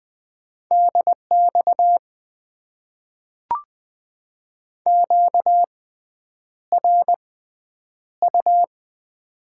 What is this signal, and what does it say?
Morse code, heard as a keyed tone, about 20 wpm, spaced as for 9 wpm, 700 Hz. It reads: DX QRU
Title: Top 90 QSO Elements - 6x Character Spacing - 20wpm